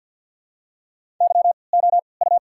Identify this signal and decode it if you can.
Morse code, heard as a keyed tone, about 54 wpm, spaced as for 29 wpm, 700 Hz. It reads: YOU
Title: 3x Character Spacing - Top 200 Words - 50wpm